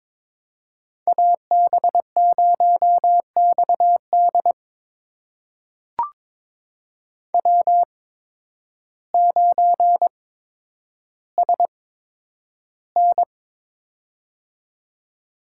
Morse code, heard as a keyed tone, about 22 wpm, spaced as for 10 wpm, 700 Hz. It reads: AB0XD W9SN
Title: USA Call Signs - 8x Character Spacing - 22wpm